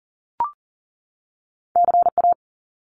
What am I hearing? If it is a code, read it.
Morse code, heard as a keyed tone, about 40 wpm, 700 Hz. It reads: CA
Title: Top Two Letter Combinations in Words 40wpm